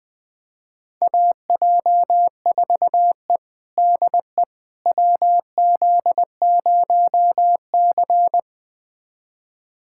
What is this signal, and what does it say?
Morse code, heard as a keyed tone, about 20 wpm, 700 Hz. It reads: AJ4E DE WZ0C
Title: USA Callsign DE Callsign - 4 Characters - 20wpm